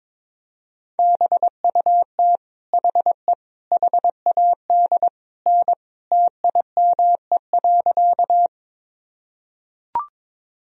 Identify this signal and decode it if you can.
Morse code, heard as a keyed tone, about 21 wpm, 700 Hz. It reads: BUT HE HAD N TIME.
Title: Sentences from Top 100 Words and Abbreviations 22wpm